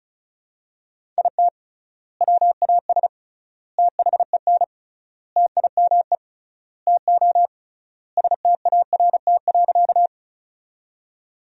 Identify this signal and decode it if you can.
Morse code, heard as a keyed tone, about 36 wpm, 700 Hz. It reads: IT WAS THEN TIME TO START.
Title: 3x Word-Spacing - Sentences from Top 300 Words - 35wpm